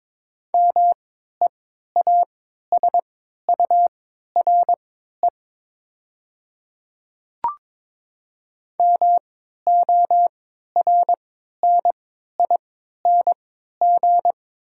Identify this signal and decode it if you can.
Morse code, heard as a keyed tone, about 22 wpm, spaced as for 13 wpm, 700 Hz. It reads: MEASURE MORNING
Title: 3x Character Spacing - Top 500 Words - 22wpm